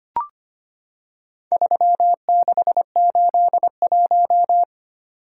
Morse code, heard as a keyed tone, about 25 wpm, 700 Hz. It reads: 3681
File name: practice / 25wpm / Four Letter-Number Combinations - 25wpm